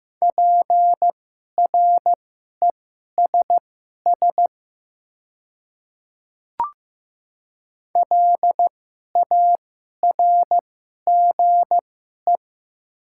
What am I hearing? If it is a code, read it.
Morse code, heard as a keyed tone, about 15 wpm, spaced as for 11 wpm, 700 Hz. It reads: PRESS LARGE
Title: 2x Character Spacing - Top 300 Words - 15wpm